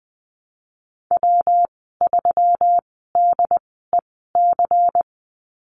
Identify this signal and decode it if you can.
Morse code, heard as a keyed tone, about 20 wpm, spaced as for 15 wpm, 700 Hz. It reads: W3DEC
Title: 122 Lesson 20g - Number 3 - Call Signs 20wpm